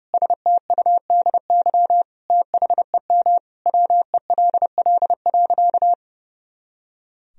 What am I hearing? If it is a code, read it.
Morse code, heard as a keyed tone, about 30 wpm, 700 Hz. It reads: STUDY THEM WELL.
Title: Sentences from Top 300 Words and Abbreviations 30wpm